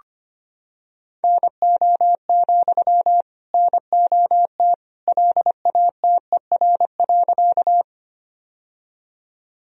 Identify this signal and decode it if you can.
Morse code, heard as a keyed tone, about 25 wpm, 700 Hz. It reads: NO, NOT LATER.
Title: Sentences from Top 1000 Words - Review Part 4 - 25wpm